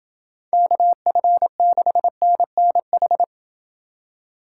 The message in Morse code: KF6NNH

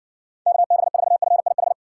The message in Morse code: KB4FIH

